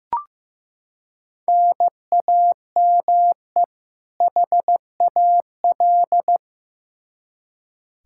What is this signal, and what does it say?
Morse code, heard as a keyed tone, about 15 wpm, 700 Hz. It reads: NAME HAL
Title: Practice QSO - Exchange 1 - Beginning - 15wpm